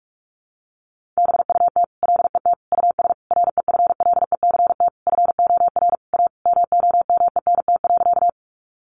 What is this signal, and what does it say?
Morse code, heard as a keyed tone, about 44 wpm, 700 Hz. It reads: BUT LET US REFLECT FOR A MOMENT.
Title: Sentences from Top 1500 Words 45wpm